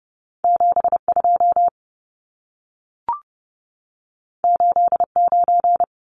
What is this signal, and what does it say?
Morse code, heard as a keyed tone, about 30 wpm, 700 Hz. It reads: 72 89
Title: Numbers - 2-Digit 30wpm